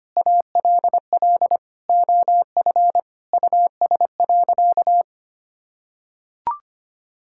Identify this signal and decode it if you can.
Morse code, heard as a keyed tone, about 25 wpm, 700 Hz. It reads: ALL OF US.
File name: practice / 25wpm / Sentences with 4-Word Phrases - 25wpm